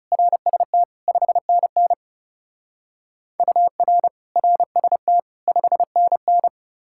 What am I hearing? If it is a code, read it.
Morse code, heard as a keyed tone, about 40 wpm, 700 Hz. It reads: RST 5NN UR RST 5NN